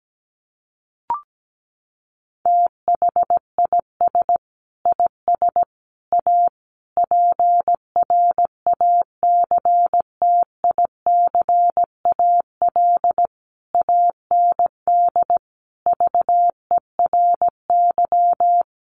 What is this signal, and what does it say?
Morse code, heard as a keyed tone, about 17 wpm, 700 Hz. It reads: THIS IS A PRACTICAL AND VERY
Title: Sentences from Top 1400 Words 17wpm